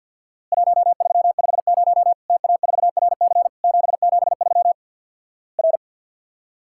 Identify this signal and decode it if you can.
Morse code, heard as a keyed tone, about 50 wpm, 700 Hz. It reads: 1350 TA4RC 773 R
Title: Worked All Europe DX Contest - QTC Exchange with Repeats - 50wpm